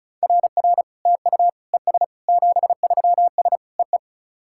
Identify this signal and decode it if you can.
Morse code, heard as a keyed tone, about 36 wpm, 690 Hz. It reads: RR TU ES 73S EE
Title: SOTA QSO Practice 35wpm